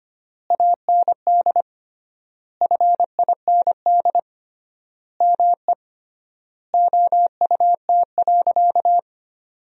Morse code, heard as a keyed tone, about 25 wpm, 700 Hz. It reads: AND FIND ME OUT.
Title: 3x Word-Spacing - Sentences from Top 500 Words - 25wpm